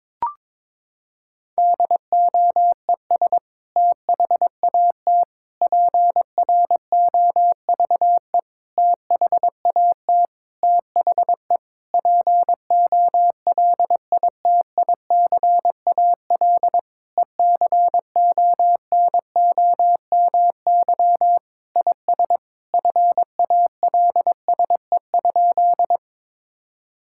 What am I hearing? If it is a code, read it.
Morse code, heard as a keyed tone, about 22 wpm, 700 Hz. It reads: DOES THAT PROVE THAT THE POLITICAL ECONOMY IS FALSE?
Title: Sentences from Top 1600 Words 22wpm